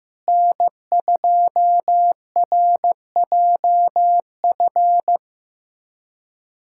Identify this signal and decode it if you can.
Morse code, heard as a keyed tone, about 16 wpm, 700 Hz. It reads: N2RJF
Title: USA Callsign DE Callsign - 5 Characters - 15wpm